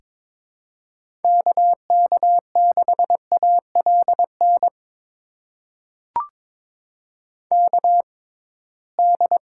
Morse code, heard as a keyed tone, about 23 wpm, spaced as for 17 wpm, 700 Hz. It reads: KK6ALN KD
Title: USA Call Signs - 6 Characters - 6x Character Spacing - 22wpm